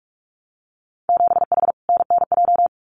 Morse code, heard as a keyed tone, about 44 wpm, 700 Hz. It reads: 7H NNJ
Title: Winter Field Day Exchanges Encore 45wpm